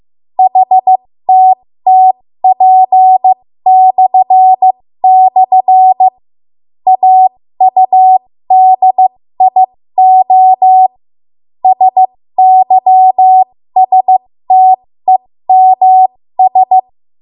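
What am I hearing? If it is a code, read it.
Morse code, heard as a keyed tone, about 15 wpm, spaced as for 12 wpm, 750 Hz. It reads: HTTP//AUDIOSYSTEMS
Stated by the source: 13 wpm Morse code practice archive transmission.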